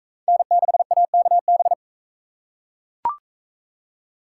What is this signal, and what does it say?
Morse code, heard as a keyed tone, about 40 wpm, 700 Hz. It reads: N6AKB